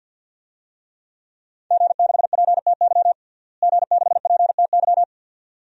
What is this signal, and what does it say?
Morse code, heard as a keyed tone, about 47 wpm, 700 Hz. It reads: G6PTY G6PTY
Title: DXCC-100 Call Signs - 5 Characters - Repeated 2x - 50wpm